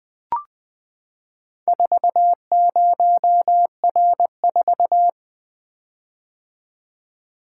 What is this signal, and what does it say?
Morse code, heard as a keyed tone, about 20 wpm, 700 Hz. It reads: 40R4
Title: Four Letter-Number Combinations - 20wpm